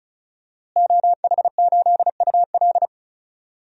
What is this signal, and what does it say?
Morse code, heard as a keyed tone, about 36 wpm, 700 Hz. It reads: OH8UL
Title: DXCC-100 Call Signs - 5 Characters - Repeated 2x - 35wpm